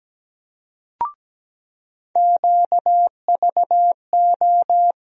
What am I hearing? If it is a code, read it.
Morse code, heard as a keyed tone, about 17 wpm, 700 Hz. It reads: QVO
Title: USA Call Sign Postfixes - 3 Letters 17wpm